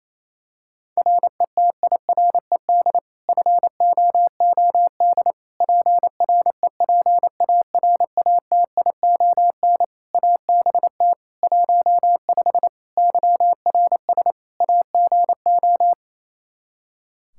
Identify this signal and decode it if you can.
Morse code, heard as a keyed tone, about 27 wpm, 700 Hz. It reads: RETIRED FOOD PREPARATION ABT 15 YRS AGO
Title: K1USN SST QSO Practice 28wpm